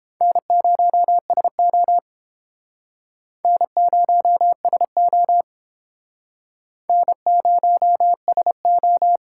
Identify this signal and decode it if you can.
Morse code, heard as a keyed tone, about 30 wpm, 700 Hz. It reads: N0SO N0SO N0SO